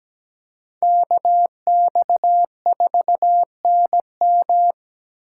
Morse code, heard as a keyed tone, about 17 wpm, 700 Hz. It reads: KX4NM
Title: K1USN SST QSO Practice - 17wpm